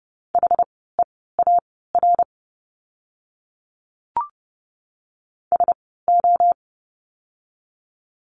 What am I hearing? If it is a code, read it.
Morse code, heard as a keyed tone, about 30 wpm, spaced as for 17 wpm, 700 Hz. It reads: HEAR SO
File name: course / 30:10wpm / 230 Lesson 37f - Letter x - Words 30wpm